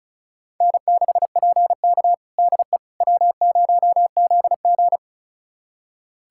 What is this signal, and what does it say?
Morse code, heard as a keyed tone, about 34 wpm, 700 Hz. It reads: N6PK DE W0ZG
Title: USA Callsign DE Callsign - 4 Characters - 35wpm